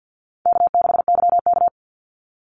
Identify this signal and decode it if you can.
Morse code, heard as a keyed tone, about 50 wpm, 700 Hz. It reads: K6YK
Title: NA SOTA Chaser Callsigns 50wpm